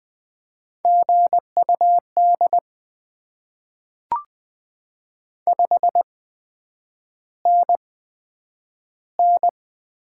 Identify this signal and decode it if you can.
Morse code, heard as a keyed tone, about 20 wpm, spaced as for 9 wpm, 700 Hz. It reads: GUD 5NN
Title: Top 20 QSO Elements - 8x Character Spacing - 20wpm